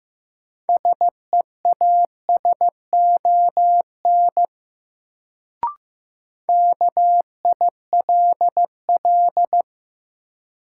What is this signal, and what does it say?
Morse code, heard as a keyed tone, about 15 wpm, 700 Hz. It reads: SEASON KILL
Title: Sentences from Top 1400 Words - New Word Drill - 15wpm